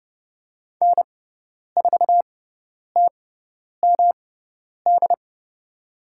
Morse code, heard as a keyed tone, about 29 wpm, spaced as for 10 wpm, 700 Hz. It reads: N4TMD
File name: Lesson 18g - Letter m - Call Signs 30wpm